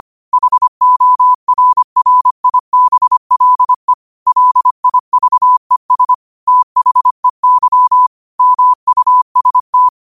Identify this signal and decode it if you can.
Morse code, heard as a keyed tone, about 25 wpm, 1000 Hz. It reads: HORRIBLE LIVES THEY MUST